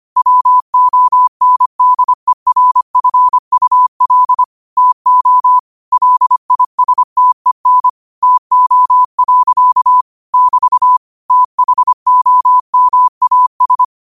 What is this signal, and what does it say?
Morse code, heard as a keyed tone, about 25 wpm, 1000 Hz. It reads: WONDERFUL TO LISTEN TO. <BT> THOMAS
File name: qotdfiles / 25wpm